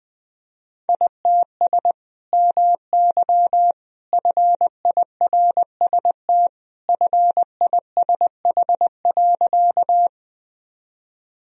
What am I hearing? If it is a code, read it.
Morse code, heard as a keyed tone, about 20 wpm, 700 Hz. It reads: ITS MY FIRST FISH.
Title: Sentences from Top 1000 Words - Review Part 1 - 20wpm